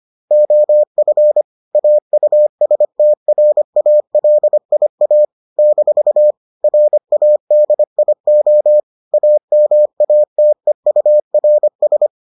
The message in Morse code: OF AUSTRALIA - RADIO AMATEURS

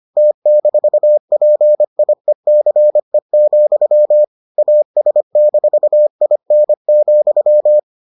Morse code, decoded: T-PIECE, AS-IN,

